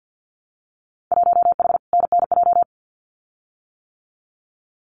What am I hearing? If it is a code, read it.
Morse code, heard as a keyed tone, about 50 wpm, 700 Hz. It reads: 1H NNJ
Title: Winter Field Day Exchanges 50wpm